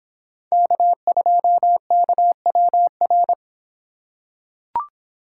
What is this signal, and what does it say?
Morse code, heard as a keyed tone, about 26 wpm, 700 Hz. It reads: K2KWR